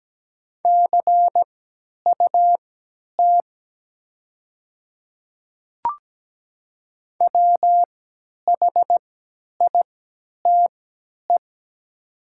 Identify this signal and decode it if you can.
Morse code, heard as a keyed tone, about 17 wpm, spaced as for 10 wpm, 700 Hz. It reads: CUT WHITE
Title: 3x Character Spacing - Top 300 Words - 17wpm